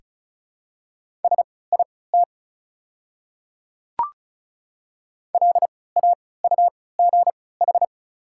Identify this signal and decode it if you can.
Morse code, heard as a keyed tone, about 36 wpm, spaced as for 20 wpm, 700 Hz. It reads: SIT LAUGH
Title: 3x Character Spacing - Top 500 Words - 35wpm